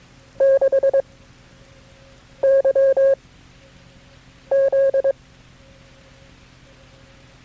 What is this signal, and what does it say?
Morse code, heard as a keyed tone, about 22 wpm, 540 Hz. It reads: 6 Y Z
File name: FreeFiles / CW-practice / KB6MT-tape3-side1